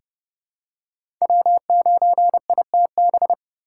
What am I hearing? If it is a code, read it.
Morse code, heard as a keyed tone, about 30 wpm, 700 Hz. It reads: W9ITB
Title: USA Call Signs - 5 Characters - 3x Character Spacing - 30wpm